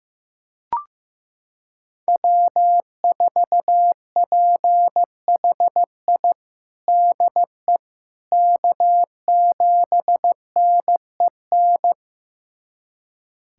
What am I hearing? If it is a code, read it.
Morse code, heard as a keyed tone, about 15 wpm, 700 Hz. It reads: W4PHI DE K7NEN
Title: USA Callsign DE Callsign - 5 Characters - 15wpm